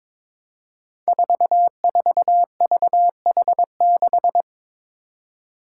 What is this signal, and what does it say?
Morse code, heard as a keyed tone, about 22 wpm, 700 Hz. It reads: 44VH6
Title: Five Challenging Letter-Number Combinations 22wpm